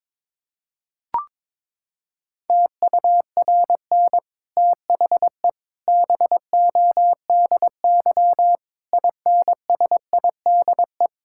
Morse code, heard as a keyed tone, about 22 wpm, 700 Hz. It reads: TURN THE BODY INSIDE
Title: Sentences from Top 1000 Words - Review Part 6 - 22wpm